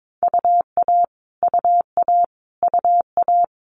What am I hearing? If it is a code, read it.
Morse code, heard as a keyed tone, about 22 wpm, 700 Hz. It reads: UA UA UA
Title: Multi-letter Phonemes Repeated 3x 22wpm